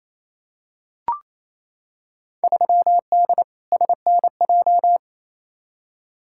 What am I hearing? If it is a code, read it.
Morse code, heard as a keyed tone, about 28 wpm, 700 Hz. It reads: 3D SNJ
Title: Field Day Exchanges Encore - 28wpm